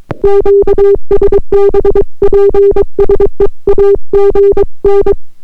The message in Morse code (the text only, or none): QSB PSE AGN